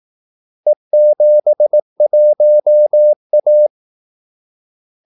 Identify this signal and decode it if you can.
Morse code, heard as a keyed tone, about 18 wpm, 600 Hz. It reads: E71A
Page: Kaldesignaler 18 wpm